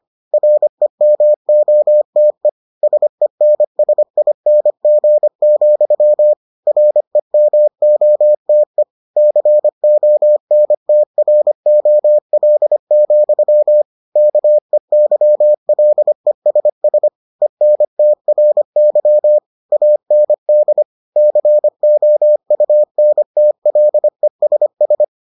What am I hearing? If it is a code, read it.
Morse code, heard as a keyed tone, about 25 wpm, 600 Hz. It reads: REMOTE SENSING, REMOTE CONTROL, KEYLESS ENTRY AND COUNTLESS